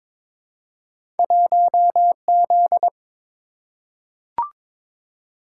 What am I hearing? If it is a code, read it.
Morse code, heard as a keyed tone, about 22 wpm, 700 Hz. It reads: 1Z
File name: practice / 22wpm / Two Letter-Number Combinations - 22wpm